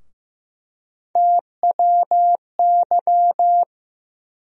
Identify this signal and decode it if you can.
Morse code, heard as a keyed tone, about 16 wpm, 700 Hz. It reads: TWY
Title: USA Call Sign Postfixes - 3 Letters 15wpm